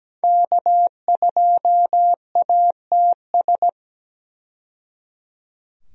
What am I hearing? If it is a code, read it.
Morse code, heard as a keyed tone, about 17 wpm, 700 Hz. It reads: K2ATS
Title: USA Call Signs - 5 Characters - 17wpm